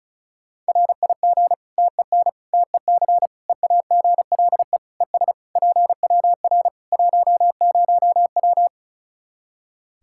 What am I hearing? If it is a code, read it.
Morse code, heard as a keyed tone, about 36 wpm, 700 Hz. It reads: RIG TEN TEC EAGLE ES PWR 10W
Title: K1USN SST QSO Practice 35wpm